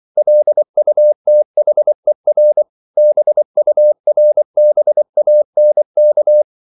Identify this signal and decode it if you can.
Morse code, heard as a keyed tone, about 24 wpm, 600 Hz. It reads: LUTHER BURBANK
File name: CW / Practice / EnglishQuotations0324